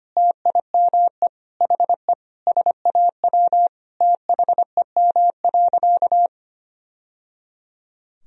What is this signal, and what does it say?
Morse code, heard as a keyed tone, about 25 wpm, 700 Hz. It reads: TIME HE SAW THEM.
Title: Sentences from Top 1000 Words - Review Part 1 - 25wpm